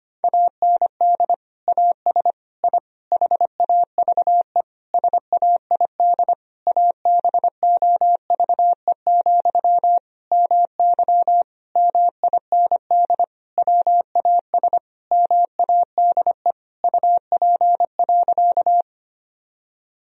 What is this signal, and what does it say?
Morse code, heard as a keyed tone, about 25 wpm, 700 Hz. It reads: AND AS I HAVE SAID ABOVE, MY MIND WAS MADE UP.
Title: Sentences from Top 1000 Words - Review Part 2 - 25wpm